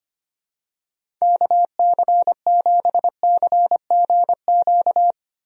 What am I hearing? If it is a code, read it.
Morse code, heard as a keyed tone, about 25 wpm, 700 Hz. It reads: KC7CGQ
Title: USA Call Signs - 6 Characters - Repeated 2x - 25wpm